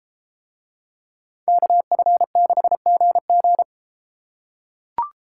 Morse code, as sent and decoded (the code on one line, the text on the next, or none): -.- ..-. -.... --. --.
KF6GG